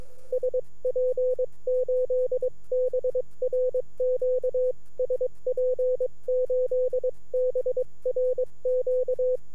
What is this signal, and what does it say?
Morse code, heard as a keyed tone, about 22 wpm, 500 Hz. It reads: SP8BRQ SP8BRQ